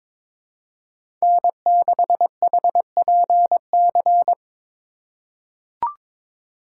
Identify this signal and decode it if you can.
Morse code, heard as a keyed tone, about 22 wpm, 700 Hz. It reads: N6HPC
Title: USA Call Signs - 5 Characters - 6x Character Spacing - 22wpm